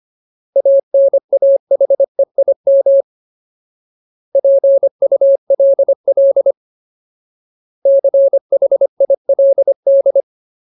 Morse code, heard as a keyed tone, about 25 wpm, 550 Hz. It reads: ANAHEIM PULL CHILD
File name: Ultimate Word List / 25WPM Ultimate